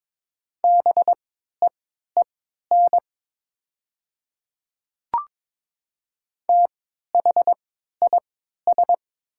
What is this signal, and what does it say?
Morse code, heard as a keyed tone, about 22 wpm, spaced as for 13 wpm, 700 Hz. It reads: BEEN THIS
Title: 3x Character Spacing - Top 100 Words - 22wpm